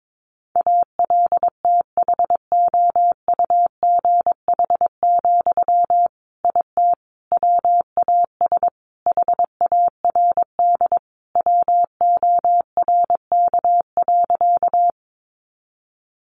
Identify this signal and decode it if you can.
Morse code, heard as a keyed tone, about 22 wpm, 700 Hz. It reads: ALTHOUGH, IT WAS HARD WORK.
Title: Sentences from Top 600 Words 22wpm